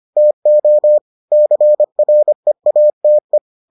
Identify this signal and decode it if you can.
Morse code, heard as a keyed tone, about 25 wpm, 600 Hz. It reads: TO CREATE